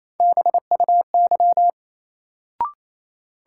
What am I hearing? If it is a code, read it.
Morse code, heard as a keyed tone, about 28 wpm, 700 Hz. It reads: BUY
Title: Sentences from Top 700 Words - New Word Drill - 28wpm